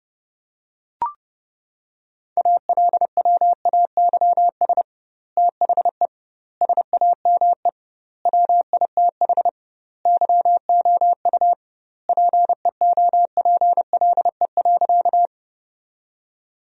Morse code, heard as a keyed tone, about 32 wpm, 700 Hz. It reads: ALWAYS THE SAME WITH YOU PEOPLE.
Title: 2x Word-Spacing - Sentences from Top 300 Words - 30wpm